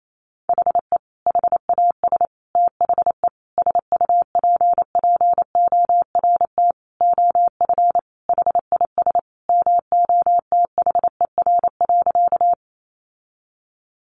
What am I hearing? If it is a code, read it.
Morse code, heard as a keyed tone, about 28 wpm, 700 Hz. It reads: HE HAS THE SUPPORT OF HIS MOTHER.
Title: Sentences from Top 800 Words 28wpm